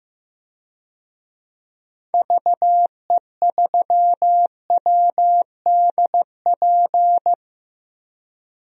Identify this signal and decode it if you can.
Morse code, heard as a keyed tone, about 15 wpm, 700 Hz. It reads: VE3WDP
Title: DXCC-100 Call Signs - 6 Characters - Repeated 3x - 15wpm